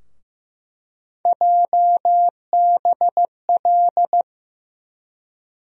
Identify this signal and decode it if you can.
Morse code, heard as a keyed tone, about 15 wpm, 700 Hz. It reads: JBL
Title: USA Call Sign Postfixes - 3 Letters 15wpm